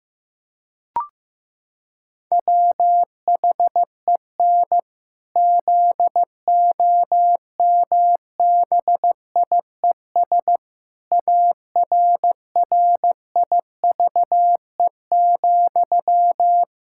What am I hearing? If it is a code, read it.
Morse code, heard as a keyed tone, about 15 wpm, 700 Hz. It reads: WHEN ZOMBIES ARRIVE,
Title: Pangram Challenge v2 - 15wpm